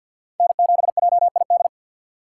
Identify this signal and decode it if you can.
Morse code, heard as a keyed tone, about 51 wpm, 700 Hz. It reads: N7JID
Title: USA Call Signs - 4x Character Spacing - 50wpm